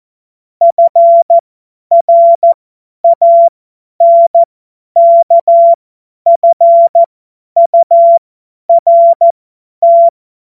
Morse code, heard as a keyed tone, about 14 wpm, spaced as for 10 wpm, 680 Hz. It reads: FRANKFURT